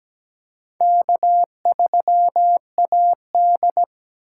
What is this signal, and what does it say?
Morse code, heard as a keyed tone, about 20 wpm, 700 Hz. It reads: K3AD
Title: Speed Racing - USA 1x2 Call Signs 17wpm